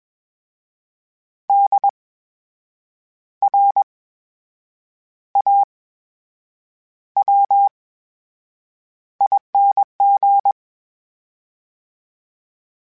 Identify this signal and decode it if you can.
Morse code, heard as a keyed tone, about 21 wpm, spaced as for 7 wpm, 800 Hz. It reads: DRAWING
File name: Special INGs-mp3s / Special INGs-mp3s / VerySlowToFaster Ing @20